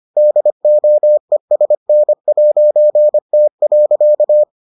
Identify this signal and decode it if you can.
Morse code, heard as a keyed tone, about 25 wpm, 600 Hz. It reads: DOESN'T.